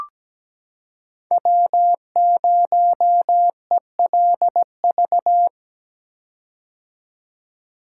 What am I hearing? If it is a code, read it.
Morse code, heard as a keyed tone, about 17 wpm, 700 Hz. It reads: W0ELV
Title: Code Groups - 5 Letter:Numbers - 17wpm